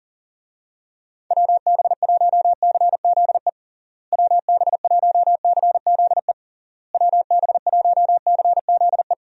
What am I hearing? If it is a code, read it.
Morse code, heard as a keyed tone, about 40 wpm, 700 Hz. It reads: WB1CZE WB1CZE WB1CZE
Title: USA Call Signs - 6 Characters - Repeated 3x - 40wpm